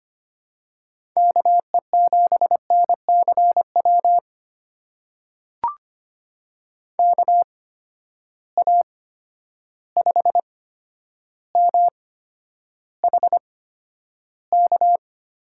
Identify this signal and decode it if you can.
Morse code, heard as a keyed tone, about 25 wpm, spaced as for 11 wpm, 700 Hz. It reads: KE7NCW KA5MHK
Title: USA Call Signs - 6 Characters - 8x Character Spacing - 25wpm